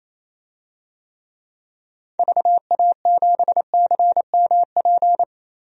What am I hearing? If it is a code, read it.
Morse code, heard as a keyed tone, about 27 wpm, 700 Hz. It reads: VA7CMP
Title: DXCC-100 Call Signs - 6 Characters - 28wpm